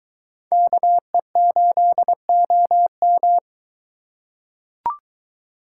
Morse code, heard as a keyed tone, about 22 wpm, 700 Hz. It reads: KE8OM